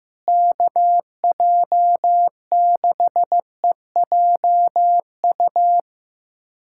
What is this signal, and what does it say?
Morse code, heard as a keyed tone, about 15 wpm, 700 Hz. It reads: KJ6EJU
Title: USA Callsign DE Callsign 15wpm